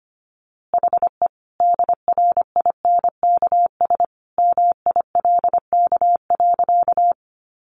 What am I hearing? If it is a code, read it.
Morse code, heard as a keyed tone, about 25 wpm, 700 Hz. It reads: HE DRINKS MILK.
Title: Easy Phrases 25wpm